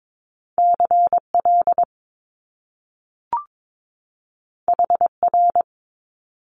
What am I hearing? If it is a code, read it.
Morse code, heard as a keyed tone, about 22 wpm, 700 Hz. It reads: CL HR
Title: Top Two Letter Combinations in Words 22wpm